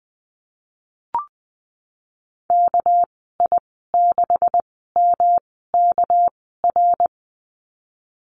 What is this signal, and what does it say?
Morse code, heard as a keyed tone, about 19 wpm, spaced as for 14 wpm, 700 Hz. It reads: KI6MKR
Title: 231 Lesson 37g - Letter x - Call Signs 20wpm